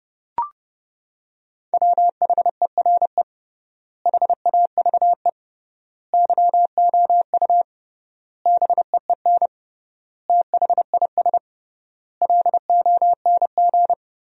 Morse code, heard as a keyed tone, about 30 wpm, 700 Hz. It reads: WHERE HAVE YOU BEEN THIS LONG
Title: 3x Word-Spacing - Sentences from Top 300 Words - 30wpm